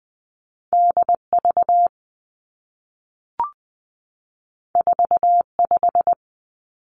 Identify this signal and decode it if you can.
Morse code, heard as a keyed tone, about 20 wpm, 700 Hz. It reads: D V 4 5
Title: Two Challenging Letter-Number Combinations 20wpm